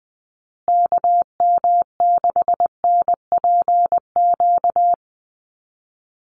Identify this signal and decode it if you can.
Morse code, heard as a keyed tone, about 20 wpm, 700 Hz. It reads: KM6NPQ
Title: USA Call Signs Repeated 3x 20wpm